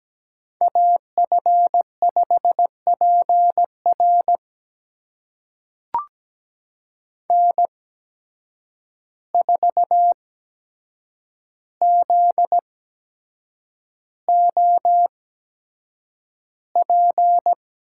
Morse code, heard as a keyed tone, about 17 wpm, spaced as for 7 wpm, 700 Hz. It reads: AF5PR N4ZOP
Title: USA Call Signs - 5 Characters - 8x Character Spacing - 17wpm